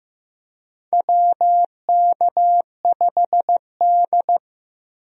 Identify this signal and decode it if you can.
Morse code, heard as a keyed tone, about 15 wpm, 700 Hz. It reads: WK5D
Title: USA Call Signs - 4 Characters - 15wpm